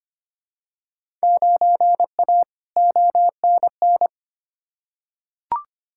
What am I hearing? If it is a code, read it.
Morse code, heard as a keyed tone, about 25 wpm, 700 Hz. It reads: 9A ONN
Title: Field Day Exchanges Encore - 25wpm